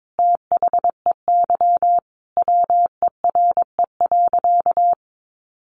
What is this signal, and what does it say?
Morse code, heard as a keyed tone, about 22 wpm, 700 Hz. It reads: THEY WERE.
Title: Sentences from Top 900 Words 22wpm